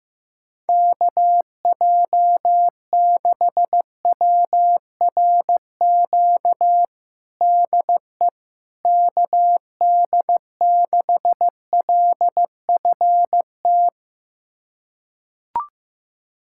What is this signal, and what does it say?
Morse code, heard as a keyed tone, about 15 wpm, 700 Hz. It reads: KJ6WRQ DE KD6LFT
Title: USA Callsign DE Callsign - 6 Characters - 15wpm